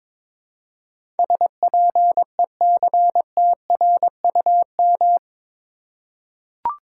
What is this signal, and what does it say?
Morse code, heard as a keyed tone, about 21 wpm, 700 Hz. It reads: SPECTRUM
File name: practice / 22wpm / Eight Letter Words - 22wpm